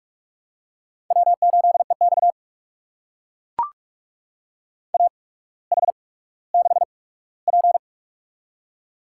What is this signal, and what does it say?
Morse code, heard as a keyed tone, about 44 wpm, spaced as for 19 wpm, 700 Hz. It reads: W8EX AH6P
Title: USA Call Signs - 4 Characters - 8x Character Spacing - 45wpm